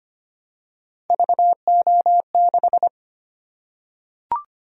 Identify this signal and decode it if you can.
Morse code, heard as a keyed tone, about 25 wpm, 700 Hz. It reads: VO6
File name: Three Letter-Number Combinations - 25wpm